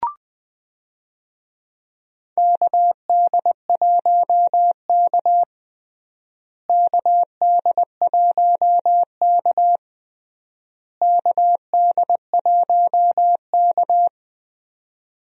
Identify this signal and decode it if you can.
Morse code, heard as a keyed tone, about 21 wpm, 690 Hz. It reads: KD1K KD1K KD1K
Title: USA Call Signs - 4 Characters - Repeated 3x - 20wpm